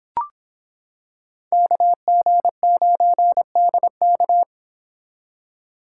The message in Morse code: KG9DK

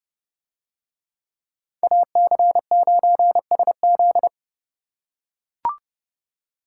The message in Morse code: AC9SZ

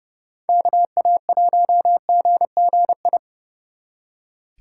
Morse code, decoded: KA1GGI

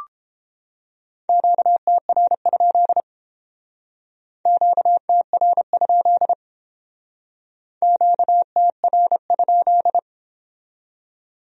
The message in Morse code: QTR? QTR? QTR?

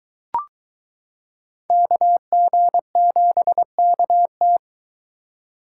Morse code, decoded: KG7KT

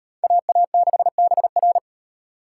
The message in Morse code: AA6BR

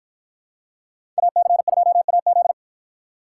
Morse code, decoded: AC2AZ

